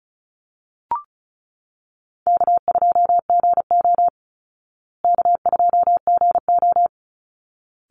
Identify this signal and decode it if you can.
Morse code, heard as a keyed tone, about 34 wpm, 700 Hz. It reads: K2GO K2GO
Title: Call Signs Repeated 3x - 35wpm